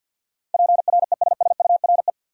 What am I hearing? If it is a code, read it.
Morse code, heard as a keyed tone, about 53 wpm, 700 Hz. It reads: PRESSURE
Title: Sentences from Top 1400 Words - New Word Drill Alt - 50wpm